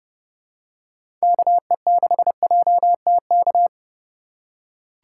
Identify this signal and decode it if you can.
Morse code, heard as a keyed tone, about 30 wpm, 700 Hz. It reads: KE6JTK
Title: USA Call Signs - 6 Characters - 4x Character Spacing - 30wpm